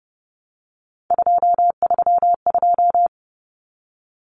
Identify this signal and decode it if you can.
Morse code, heard as a keyed tone, about 30 wpm, 700 Hz. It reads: 232
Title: Common Cut Numbers - 3-Digit 30wpm